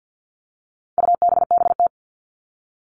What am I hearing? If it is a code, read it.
Morse code, heard as a keyed tone, about 50 wpm, 700 Hz. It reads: UBBT
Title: All Cut Numbers - 4-Digit 50wpm